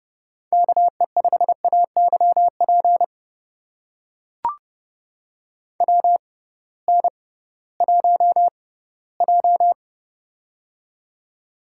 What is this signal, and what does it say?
Morse code, heard as a keyed tone, about 30 wpm, spaced as for 18 wpm, 700 Hz. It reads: KE5AYP WN1J
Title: USA Call Signs - 6x Character Spacing - 30wpm